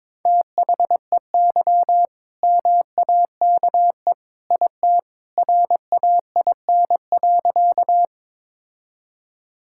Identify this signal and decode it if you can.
Morse code, heard as a keyed tone, about 22 wpm, 690 Hz. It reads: THEY MAKE IT RAIN.
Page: morse-code-ninja-practice-22wpm